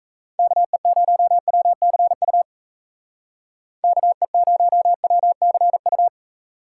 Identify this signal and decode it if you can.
Morse code, heard as a keyed tone, about 34 wpm, 700 Hz. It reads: KE0WCU KE0WCU
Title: Speed Racing - USA 2x3 Call Signs 28wpm